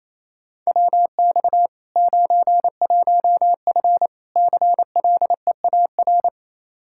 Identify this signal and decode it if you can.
Morse code, heard as a keyed tone, about 27 wpm, 700 Hz. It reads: WX 91F CLEAR
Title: Practice QSO - Exchange 2 - Info - 28wpm